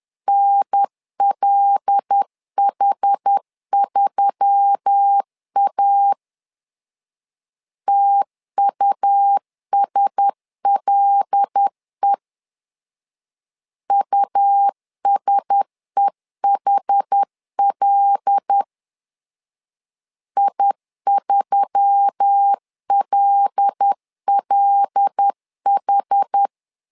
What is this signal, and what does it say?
Morse code, heard as a keyed tone, about 10 wpm, 800 Hz. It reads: NLH3A TUSLE USEHL I3LLH
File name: LEZIONI / 5 LBVC / 5 L 10wpm 2259